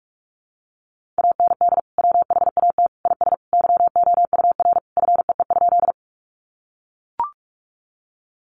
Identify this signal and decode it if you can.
Morse code, heard as a keyed tone, about 44 wpm, 700 Hz. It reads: AND WHAT IS YOUR FEE?
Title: Sentences from Top 1500 Words 45wpm